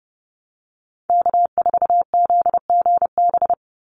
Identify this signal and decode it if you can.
Morse code, heard as a keyed tone, about 30 wpm, 700 Hz. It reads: K4ZGB
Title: Call Signs Repeated 2x - 30wpm